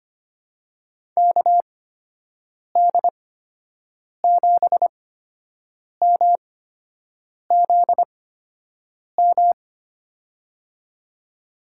Morse code, heard as a keyed tone, about 25 wpm, spaced as for 7 wpm, 700 Hz. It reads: KD7MZM
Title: USA Call Signs - 8x Character Spacing - 25wpm